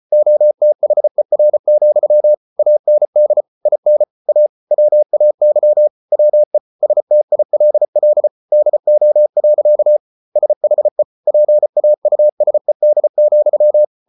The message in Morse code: OTHER, AND IN A WAY WE STILL DO. SHE PAUSED,

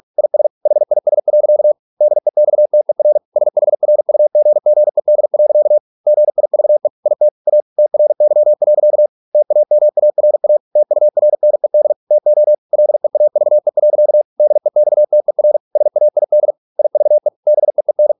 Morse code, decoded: IS HIS. DEXTER SHRUGGED. GIVE IT A TRY. TAMARA TURNED TO LEAVE. DEXTER SAID IVE BEEN

